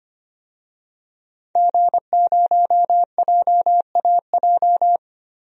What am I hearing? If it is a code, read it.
Morse code, heard as a keyed tone, about 25 wpm, 700 Hz. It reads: G0JAJ
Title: DXCC-100 Call Signs - 5 Characters - 25wpm